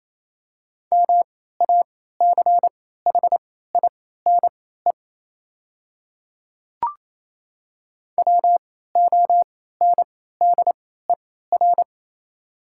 Morse code, heard as a keyed tone, about 28 wpm, spaced as for 16 wpm, 700 Hz. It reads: MACHINE WONDER
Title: 3x Character Spacing - Top 500 Words - 28wpm